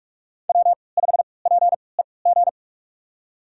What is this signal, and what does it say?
Morse code, heard as a keyed tone, about 46 wpm, spaced as for 26 wpm, 700 Hz. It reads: W5PEG
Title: USA Call Signs - 3x Character Spacing - 45wpm